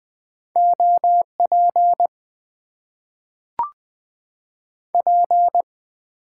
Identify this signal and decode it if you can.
Morse code, heard as a keyed tone, about 20 wpm, 700 Hz. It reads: OP P
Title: Top 90 QSO Elements - 8x Character Spacing - 20wpm